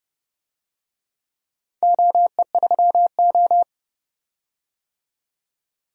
Morse code, heard as a keyed tone, about 30 wpm, 700 Hz. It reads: OE3O
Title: DXCC-100 Call Signs - 4 Characters - 30wpm